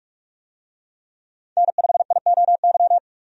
Morse code, heard as a keyed tone, about 46 wpm, 700 Hz. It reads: N5IOY
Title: USA Call Signs - 5 Characters - 45wpm